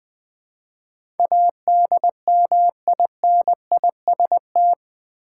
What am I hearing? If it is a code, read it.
Morse code, heard as a keyed tone, about 20 wpm, 700 Hz. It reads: ADMINIST
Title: Build A Word - 4 Letters At A Time - 10-Letter Words - 20wpm